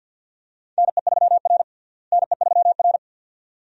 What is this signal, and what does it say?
Morse code, heard as a keyed tone, about 52 wpm, 700 Hz. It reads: NE3R NE3R
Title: USA Call Signs - 4 Characters - Repeated 2x - 50wpm